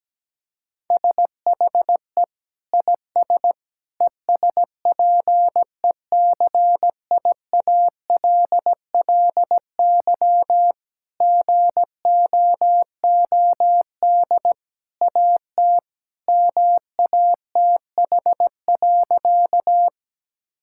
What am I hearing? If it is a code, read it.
Morse code, heard as a keyed tone, about 17 wpm, 700 Hz. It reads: SHE IS ESPECIALLY GOOD AT MATH.
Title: Sentences from Top 1000 Words - Review Part 6 - 17wpm